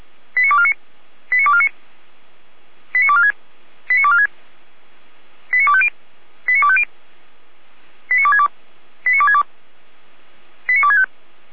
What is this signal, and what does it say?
Five-tone selective call (ZVEI), heard as quick runs of five tones, tones of about 70 ms. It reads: 89249 89249 89246 89246 89250 89250 89262 89262 89265